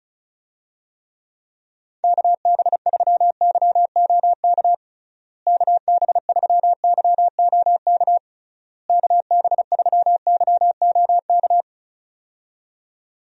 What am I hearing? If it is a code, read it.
Morse code, heard as a keyed tone, about 36 wpm, 700 Hz. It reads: KB3YOK KB3YOK KB3YOK
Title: USA Call Signs - 6 Characters - Repeated 3x - 35wpm